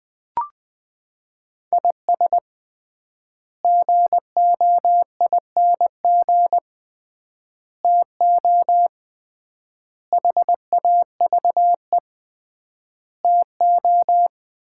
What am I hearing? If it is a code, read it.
Morse code, heard as a keyed tone, about 21 wpm, 700 Hz. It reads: IS GOING TO HAVE TO
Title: 3x Word-Spacing - Sets of 5 Words - 20wpm